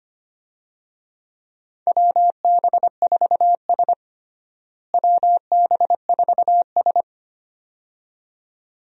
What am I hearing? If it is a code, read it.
Morse code, heard as a keyed tone, about 25 wpm, 700 Hz. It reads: WB4S WB4S
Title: USA Call Signs - 4 Characters - Repeated 2x - 25wpm